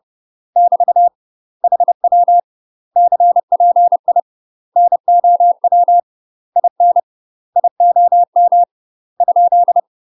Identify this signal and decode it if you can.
Morse code, heard as a keyed tone, about 30 wpm, 700 Hz. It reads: <BT> HW CPI NOW IN IOM ?
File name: UK MASTER QSOs TRAINING / 30WPM / UK MASTER QSOs 0039